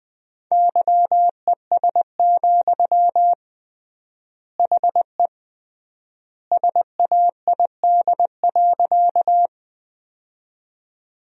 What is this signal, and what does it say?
Morse code, heard as a keyed tone, about 21 wpm, 700 Hz. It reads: YES, HE SAID.
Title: 3x Word-Spacing - Sentences from Top 500 Words - 20wpm